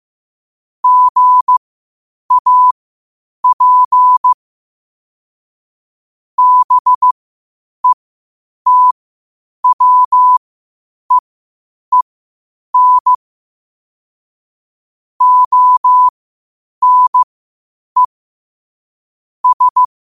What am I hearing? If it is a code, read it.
Morse code, heard as a keyed tone, about 15 wpm, spaced as for 8 wpm, 1000 Hz. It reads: GAP BETWEEN ONE S